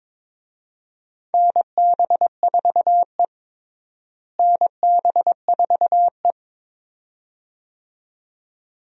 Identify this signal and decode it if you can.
Morse code, heard as a keyed tone, about 22 wpm, 700 Hz. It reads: NB4E NB4E
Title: USA Call Signs - 4 Characters - Repeated 2x - 22wpm